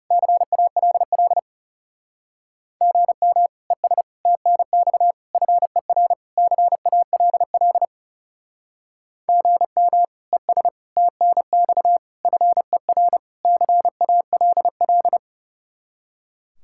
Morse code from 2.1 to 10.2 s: GM ES TNX FER CALL GM

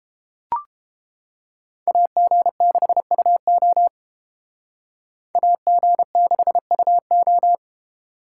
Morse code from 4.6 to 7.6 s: AG6UO

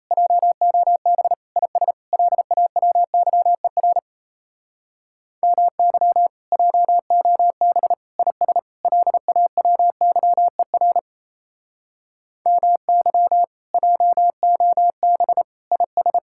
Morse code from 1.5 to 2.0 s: IS

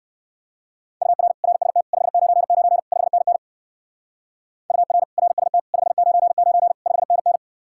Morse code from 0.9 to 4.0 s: UR RST 599 5NN

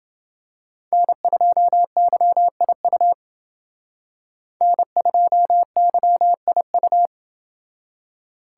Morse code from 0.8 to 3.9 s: N2YIU

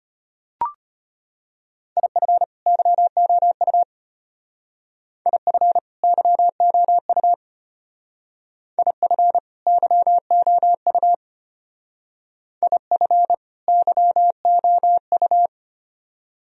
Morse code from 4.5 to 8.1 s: IF YOU

